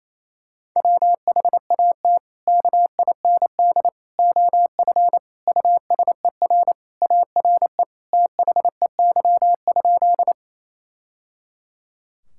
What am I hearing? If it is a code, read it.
Morse code, heard as a keyed tone, about 28 wpm, 700 Hz. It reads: WHAT KIND OF USER ARE THEY?
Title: Sentences from Top 1000 Words - Review Part 5 - 28wpm